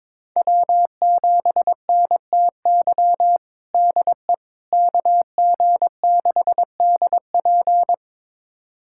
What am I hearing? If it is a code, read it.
Morse code, heard as a keyed tone, about 22 wpm, 700 Hz. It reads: W7NTY DE KG6DP
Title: USA Callsign DE Callsign - 5 Characters - 22wpm